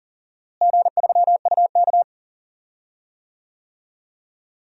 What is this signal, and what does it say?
Morse code, heard as a keyed tone, about 40 wpm, 700 Hz. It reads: G3UK
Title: DXCC-100 Call Signs - 4 Characters - Repeated 3x - 40wpm